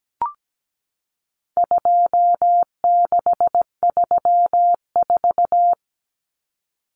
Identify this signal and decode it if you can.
Morse code, heard as a keyed tone, about 17 wpm, 700 Hz. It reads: 2634
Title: Numbers - 4-Digit 17wpm